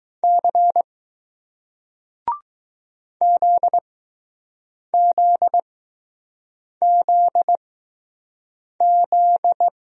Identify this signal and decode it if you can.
Morse code, heard as a keyed tone, about 20 wpm, spaced as for 6 wpm, 700 Hz. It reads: C ZZZZ